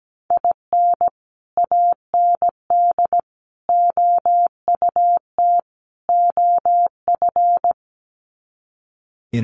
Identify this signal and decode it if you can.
Morse code, heard as a keyed tone, about 17 wpm, 700 Hz. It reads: IN AND OUT OF
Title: Sets of 4 Words 17wpm